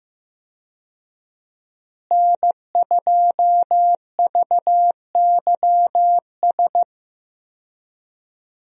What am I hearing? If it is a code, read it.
Morse code, heard as a keyed tone, about 15 wpm, 700 Hz. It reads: N2VYS
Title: USA Call Signs - 5 Characters - Repeated 2x - 15wpm